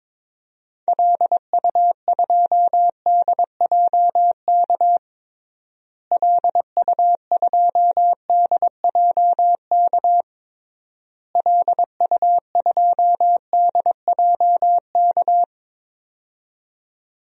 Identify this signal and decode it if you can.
Morse code, heard as a keyed tone, about 22 wpm, 700 Hz. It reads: LU2DJK LU2DJK LU2DJK
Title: DXCC-100 Call Signs - 6 Characters - Repeated 3x - 22wpm